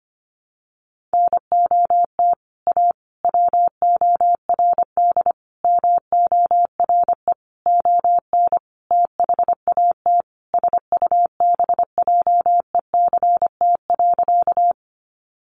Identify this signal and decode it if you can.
Morse code, heard as a keyed tone, about 25 wpm, 700 Hz. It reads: NOT A WORD MORE ON THAT SUBJECT.
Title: Sentences from Top 800 Words 25wpm